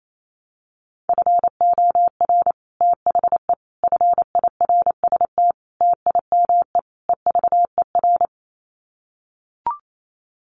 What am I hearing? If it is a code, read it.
Morse code, heard as a keyed tone, about 28 wpm, 700 Hz. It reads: FOR THE FIRST TIME EVER
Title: Sets of 5 Words Encore2 28wpm